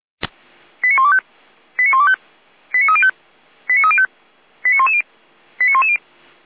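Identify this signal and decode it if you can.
Five-tone selective call (ZVEI), heard as quick runs of five tones, tones of about 80 ms. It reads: 89125 89125 89305 89305 89110 89110
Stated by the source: ZVEI five-tone selective call 89042 received by SDR.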